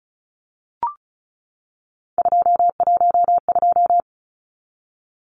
Morse code, heard as a keyed tone, about 34 wpm, 700 Hz. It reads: 212
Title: Numbers - 3-Digit 35wpm